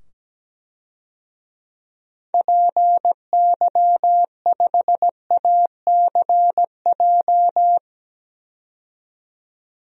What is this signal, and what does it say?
Morse code, heard as a keyed tone, about 17 wpm, 700 Hz. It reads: PY5ACJ
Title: DXCC-100 Call Signs - 6 Characters - 17wpm